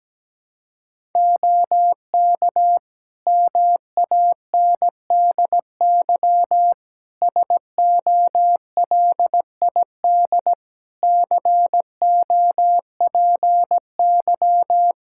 Morse code, heard as a keyed tone, about 17 wpm, 700 Hz. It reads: OK MANDY SOLID COPY
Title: Practice QSO - Exchange 3 - More Info - 17wpm